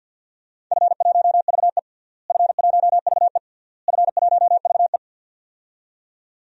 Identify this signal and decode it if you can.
Morse code, heard as a keyed tone, about 49 wpm, 700 Hz. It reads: F1VE F1VE F1VE
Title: DXCC-100 Call Signs - 4 Characters - Repeated 3x - 50wpm